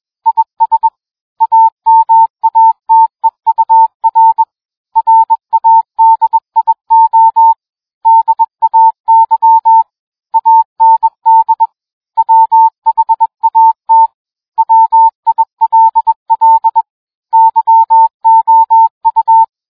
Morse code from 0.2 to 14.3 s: IS AMATEUR RADIO DAY AND WHAT